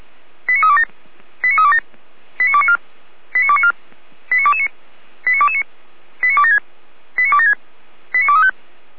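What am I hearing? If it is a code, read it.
Five-tone selective call (ZVEI), heard as quick runs of five tones, tones of about 70 ms. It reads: 89327 89327 89294 89294 89229 89229 89276 89276 89235
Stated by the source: ZVEI five-tone call 89356 received by SDR.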